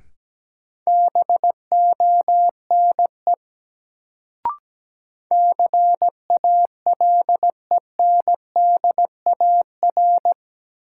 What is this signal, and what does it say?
Morse code, heard as a keyed tone, about 17 wpm, 700 Hz. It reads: BONE CALENDAR
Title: Sentences from Top 1600 Words - New Word Drill Alt - 17wpm